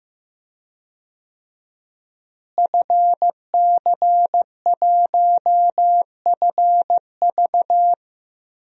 Morse code, heard as a keyed tone, about 15 wpm, 700 Hz. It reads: FC1FV
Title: DXCC-100 Call Signs - 5 Characters - Repeated 3x - 15wpm